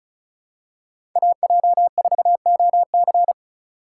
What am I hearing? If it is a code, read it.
Morse code, heard as a keyed tone, about 34 wpm, 700 Hz. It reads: AJ4OC
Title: USA Call Signs - 5 Characters - 2x Character Spacing - 35wpm